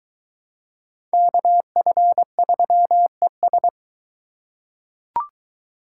Morse code, heard as a keyed tone, about 21 wpm, 700 Hz. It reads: KF3ES